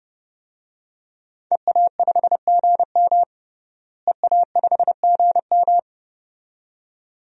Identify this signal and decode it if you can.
Morse code, heard as a keyed tone, about 32 wpm, 700 Hz. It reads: EA5GM EA5GM
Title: DXCC-100 Call Signs - 5 Characters - Repeated 2x - 30wpm